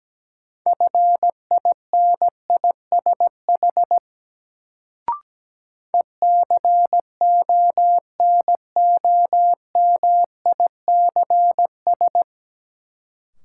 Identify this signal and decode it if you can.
Morse code, heard as a keyed tone, about 17 wpm, 700 Hz. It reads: FINISH ECONOMICS
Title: Sentences from Top 1000 Words - Review Part 3 - New Word Drill - 17wpm